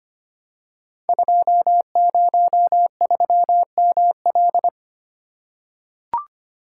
Morse code, heard as a keyed tone, about 25 wpm, 700 Hz. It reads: 203ML
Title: Code Groups - 5 Letter:Numbers - 25wpm